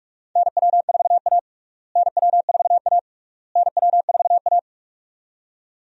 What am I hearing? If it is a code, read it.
Morse code, heard as a keyed tone, about 44 wpm, 700 Hz. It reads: NW4A NW4A NW4A
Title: USA Call Signs - 4 Characters - Repeated 3x - 45wpm